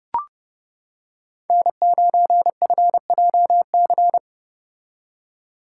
Morse code, heard as a keyed tone, about 29 wpm, 700 Hz. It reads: N9FJC